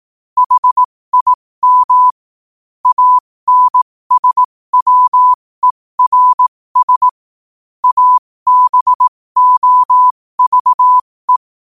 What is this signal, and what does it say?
Morse code, heard as a keyed tone, about 18 wpm, spaced as for 15 wpm, 1000 Hz. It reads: HIM ANSWERS ABOVE
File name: qotdfiles / 15wpm